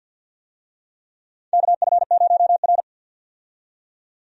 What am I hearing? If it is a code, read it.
Morse code, heard as a keyed tone, about 52 wpm, 700 Hz. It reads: KF0R
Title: USA Call Signs - 4 Characters - 50wpm